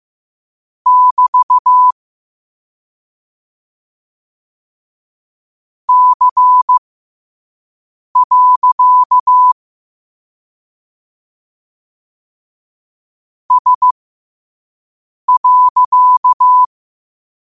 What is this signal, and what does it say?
Morse code, heard as a keyed tone, about 15 wpm, spaced as for 5 wpm, 1000 Hz. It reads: <BT> C. S.
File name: qotdfiles / 5wpm